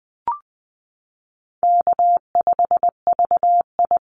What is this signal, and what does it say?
Morse code, heard as a keyed tone, about 20 wpm, 700 Hz. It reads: K5VI
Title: Call Signs Repeated 2x - 20wpm